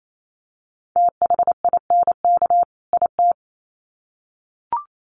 Morse code, heard as a keyed tone, about 28 wpm, 700 Hz. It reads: THINK IT
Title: Sets of 2 Words 28wpm